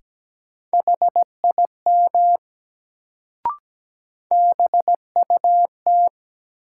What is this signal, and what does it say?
Morse code, heard as a keyed tone, about 17 wpm, 700 Hz. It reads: HIM BUT
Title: Sentences from Top 100 Words - New Word Drill - 17wpm